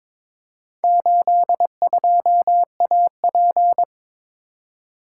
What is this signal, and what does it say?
Morse code, heard as a keyed tone, about 22 wpm, 700 Hz. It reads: 82AP
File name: practice / 22wpm / Four Letter-Number Combinations - 22wpm